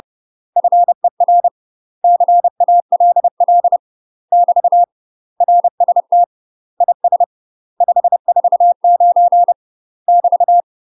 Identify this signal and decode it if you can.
Morse code, heard as a keyed tone, about 30 wpm, 700 Hz. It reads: FER CALL <BT> RST IS 549 <BT>
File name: UK MASTER QSOs TRAINING/30WPM